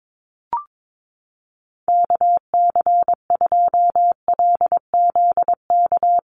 Morse code, heard as a keyed tone, about 22 wpm, 700 Hz. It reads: KC2LZK
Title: USA Call Signs 22wpm